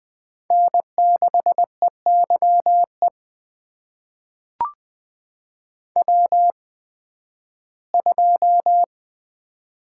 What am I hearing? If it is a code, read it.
Morse code, heard as a keyed tone, about 20 wpm, spaced as for 13 wpm, 700 Hz. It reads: N6EYE W2
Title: USA Call Signs - 5 Characters - 8x Character Spacing - 20wpm